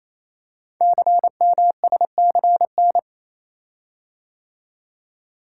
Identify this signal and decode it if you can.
Morse code, heard as a keyed tone, about 29 wpm, 700 Hz. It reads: CMSCN
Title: Code Groups - 5 Letters - 28wpm